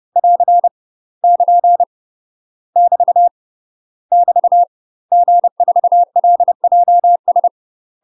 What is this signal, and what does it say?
Morse code, heard as a keyed tone, about 30 wpm, 700 Hz. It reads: <AR> <KN> <BT> <BT> G4LJS
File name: UK MASTER QSOs TRAINING / 30WPM / UK MASTER QSOs 0026